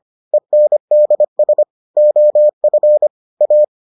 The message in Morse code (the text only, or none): ENDS OF A